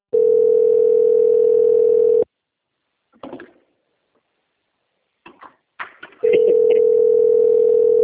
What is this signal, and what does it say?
Ringback tone (North America), 440 + 480 Hz.
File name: out- - - - - .5718